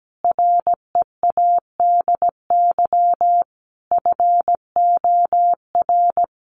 Morse code, heard as a keyed tone, about 17 wpm, 700 Hz. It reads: READY FOR
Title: Sentences from Top 1500 Words 17wpm